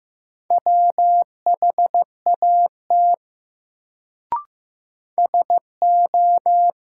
Morse code, heard as a keyed tone, about 15 wpm, 700 Hz. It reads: WHAT SO
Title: Sentences from Top 100 Words - New Word Drill - 15wpm